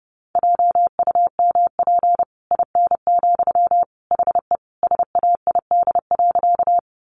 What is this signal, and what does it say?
Morse code, heard as a keyed tone, about 28 wpm, 700 Hz. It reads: JUMP IN, HE SAID.
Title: Sentences from Top 1300 Words 30wpm